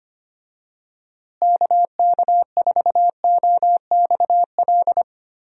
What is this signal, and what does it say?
Morse code, heard as a keyed tone, about 25 wpm, 700 Hz. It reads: KK4OXL
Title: USA Call Signs - 6 Characters - Repeated 2x - 25wpm